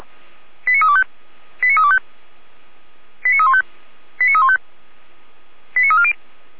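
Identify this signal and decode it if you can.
Five-tone selective call (ZVEI), heard as quick runs of five tones, tones of about 70 ms. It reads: 89325 89325 89315 89315 89340
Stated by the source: ZVEI five-tone call received with an SDR.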